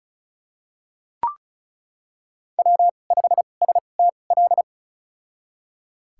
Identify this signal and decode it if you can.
Morse code, heard as a keyed tone, about 34 wpm, spaced as for 25 wpm, 700 Hz. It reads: W5STL
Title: USA Call Signs - 5 Characters - 2x Character Spacing - 35wpm